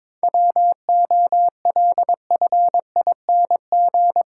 Morse code, heard as a keyed tone, about 22 wpm, 700 Hz. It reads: WOLFING
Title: Pangram Challenge v2 - 22wpm